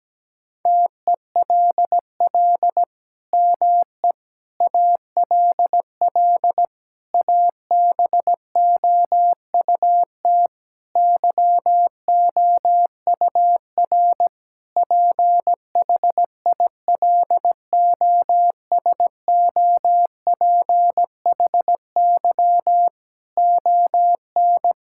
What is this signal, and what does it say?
Morse code, heard as a keyed tone, about 17 wpm, 700 Hz. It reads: TELL ME ALL ABOUT YOUR PHILOSOPHY ON
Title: Sentences from Top 1000 Words - Review Part 6 - 17wpm